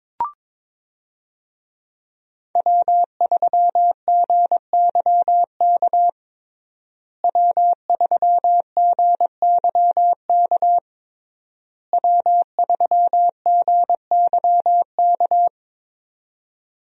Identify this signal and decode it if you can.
Morse code, heard as a keyed tone, about 22 wpm, 700 Hz. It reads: W3GYK W3GYK W3GYK
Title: USA Call Signs - 5 Characters - Repeated 3x - 22wpm